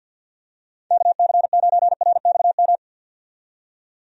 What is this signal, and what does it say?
Morse code, heard as a keyed tone, about 51 wpm, 700 Hz. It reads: KC9RXM
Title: USA Call Signs - 6 Characters - 50wpm